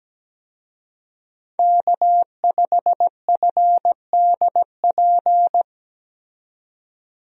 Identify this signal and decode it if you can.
Morse code, heard as a keyed tone, about 17 wpm, 700 Hz. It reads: K5FDP
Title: USA Call Signs - 5 Characters - 17wpm